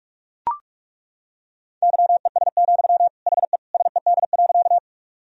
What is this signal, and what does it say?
Morse code, heard as a keyed tone, about 41 wpm, 700 Hz. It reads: YES, HE SED.
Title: Sentences from Top 500 Words and Abbreviations 45wpm